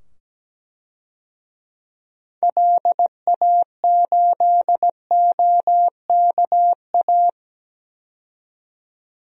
Morse code, heard as a keyed tone, about 17 wpm, 700 Hz. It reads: LA8OKA
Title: DXCC-100 Call Signs - 6 Characters - 17wpm